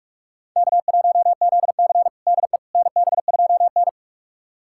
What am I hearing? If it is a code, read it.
Morse code, heard as a keyed tone, about 46 wpm, 700 Hz. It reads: K1ZC DE NB2N
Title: USA Callsign DE Callsign - 4 Characters - 45wpm